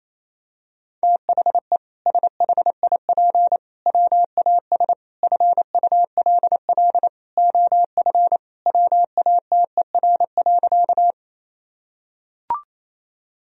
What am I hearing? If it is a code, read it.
Morse code, heard as a keyed tone, about 27 wpm, 700 Hz. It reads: THE SHIP WAS FULL OF WATER.
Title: Sentences from Top 1000 Words - Review Part 3 - 28wpm